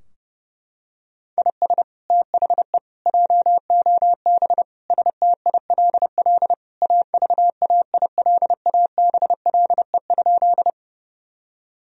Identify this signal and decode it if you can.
Morse code, heard as a keyed tone, about 30 wpm, 700 Hz. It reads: IS THE JOB STILL AVAILABLE?
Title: Sentences from Top 1000 Words - Review Part 3 - 30wpm